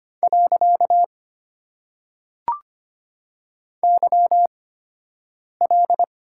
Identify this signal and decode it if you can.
Morse code, heard as a keyed tone, about 25 wpm, spaced as for 7 wpm, 700 Hz. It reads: . YL
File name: Top 90 QSO Elements - 8x Character Spacing - 25wpm